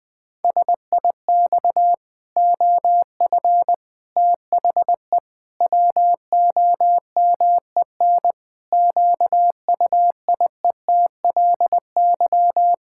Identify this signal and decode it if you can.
Morse code, heard as a keyed tone, about 20 wpm, 700 Hz. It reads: SIX OF THE WOMEN QUIETLY
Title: Pangram Challenge v2 - 20wpm